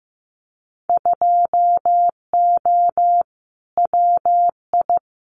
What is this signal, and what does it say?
Morse code, heard as a keyed tone, about 15 wpm, 700 Hz. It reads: 2O WI
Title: Winter Field Day Exchanges 15wpm